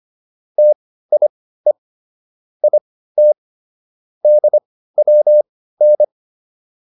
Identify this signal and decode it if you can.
Morse code, heard as a keyed tone, about 25 wpm, spaced as for 15 wpm, 600 Hz. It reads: TIE IT DWN